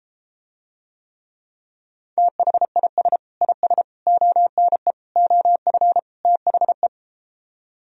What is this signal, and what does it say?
Morse code, heard as a keyed tone, about 33 wpm, 700 Hz. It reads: THIS IS ONE OF THE